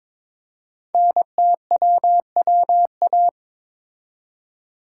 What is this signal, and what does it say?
Morse code, heard as a keyed tone, about 22 wpm, 700 Hz. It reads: NTWWA
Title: Code Groups - 5 Letters - 22wpm